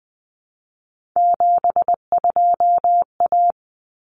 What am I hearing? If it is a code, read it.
Morse code, heard as a keyed tone, about 20 wpm, 700 Hz. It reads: 72A
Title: Common Cut Numbers - 3-Digit 20wpm